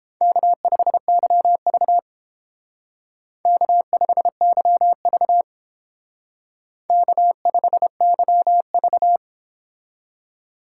Morse code, heard as a keyed tone, about 29 wpm, 700 Hz. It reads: K5YV K5YV K5YV